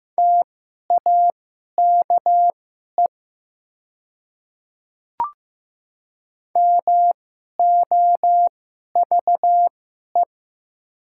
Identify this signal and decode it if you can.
Morse code, heard as a keyed tone, about 15 wpm, spaced as for 11 wpm, 700 Hz. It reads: TAKE MOVE
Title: 2x Character Spacing - Top 200 Words - 15wpm